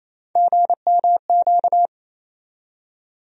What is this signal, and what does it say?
Morse code, heard as a keyed tone, about 28 wpm, 700 Hz. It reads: GMQ
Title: USA Call Sign Postfixes - 3 Letters 28wpm